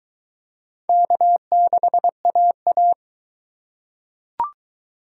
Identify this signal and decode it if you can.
Morse code, heard as a keyed tone, about 20 wpm, 700 Hz. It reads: K6AA